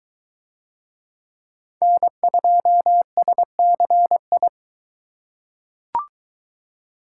Morse code, heard as a keyed tone, about 21 wpm, 700 Hz. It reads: N2SCI